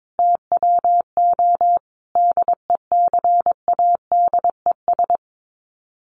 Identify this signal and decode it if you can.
Morse code, heard as a keyed tone, about 22 wpm, 700 Hz. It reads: TWO DECADES
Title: Sets of 5 Words Encore2 22wpm